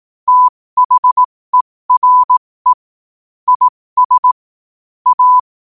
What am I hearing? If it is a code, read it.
Morse code, heard as a keyed tone, about 18 wpm, spaced as for 15 wpm, 1000 Hz. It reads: THERE IS A
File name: qotdfiles / 15wpm